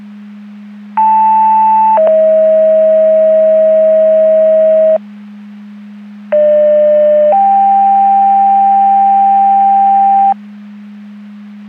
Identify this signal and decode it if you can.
Two-tone page, heard as a one-second tone, then a three-second tone, about 880 then 640 Hz, about 600 then 790 Hz. It is heard from 1.0 to 10.3 s.
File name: Portage 154.385 Paging